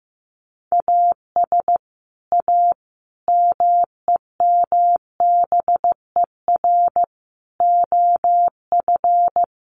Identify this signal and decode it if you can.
Morse code, heard as a keyed tone, about 15 wpm, 700 Hz. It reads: AS A MEMBER OF
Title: Sets of 5 Words 15wpm